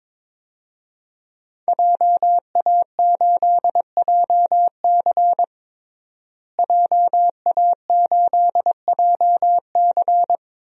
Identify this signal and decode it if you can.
Morse code, heard as a keyed tone, about 22 wpm, 700 Hz. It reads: JA8JC JA8JC
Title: DXCC-100 Call Signs - 5 Characters - Repeated 2x - 22wpm